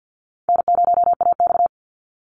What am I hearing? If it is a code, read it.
Morse code, heard as a keyed tone, about 50 wpm, 700 Hz. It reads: N0AX
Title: Call Signs Repeated 2x - 50wpm